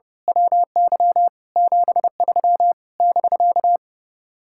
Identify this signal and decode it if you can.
Morse code, heard as a keyed tone, about 28 wpm, 700 Hz. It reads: WY 73 <BK>
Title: Speed-Racing Common QSO Phrases - 30wpm